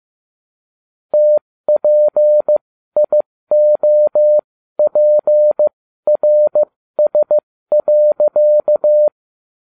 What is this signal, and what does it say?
Morse code, heard as a keyed tone, about 15 wpm, spaced as for 13 wpm, 600 Hz. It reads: TPIOPRS.